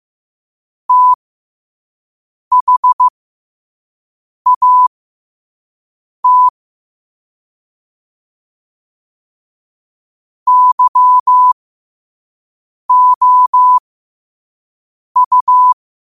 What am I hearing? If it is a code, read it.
Morse code, heard as a keyed tone, about 15 wpm, spaced as for 5 wpm, 1000 Hz. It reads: THAT YOU
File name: qotdfiles / 5wpm